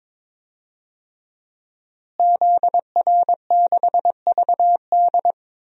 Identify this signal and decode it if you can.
Morse code, heard as a keyed tone, about 22 wpm, 700 Hz. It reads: ZR6VD
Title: DXCC-100 Call Signs - 5 Characters - Repeated 2x - 22wpm